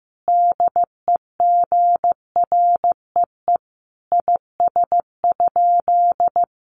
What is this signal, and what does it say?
Morse code, heard as a keyed tone, about 15 wpm, 700 Hz. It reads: DEGREE IS?
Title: Sentences from Top 1500 Words 15wpm